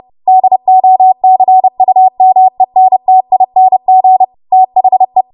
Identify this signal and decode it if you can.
Morse code, heard as a keyed tone, about 30 wpm, 750 Hz. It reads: DOCUMENTING THE
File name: Morse / 30WPM